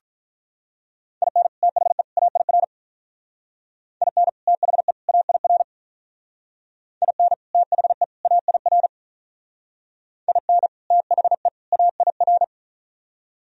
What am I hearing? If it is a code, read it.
Morse code, heard as a keyed tone, about 46 wpm, 700 Hz. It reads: IN THE AIR IN THE AIR IN THE AIR IN THE AIR